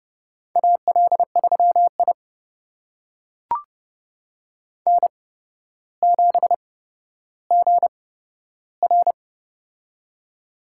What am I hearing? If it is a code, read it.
Morse code, heard as a keyed tone, about 32 wpm, spaced as for 13 wpm, 700 Hz. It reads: AL3I N7GR
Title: USA Call Signs - 4 Characters - 8x Character Spacing - 30wpm